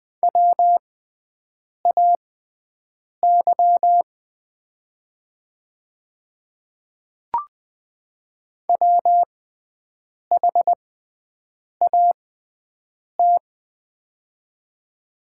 Morse code, heard as a keyed tone, about 20 wpm, spaced as for 7 wpm, 700 Hz. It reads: WAY WHAT
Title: 6x Character Spacing - Top 100 Words - 20wpm